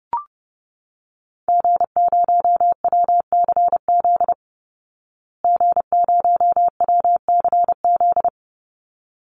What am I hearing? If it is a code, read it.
Morse code, heard as a keyed tone, about 30 wpm, 700 Hz. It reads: G0WCZ G0WCZ
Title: Call Signs Repeated 3x - 30wpm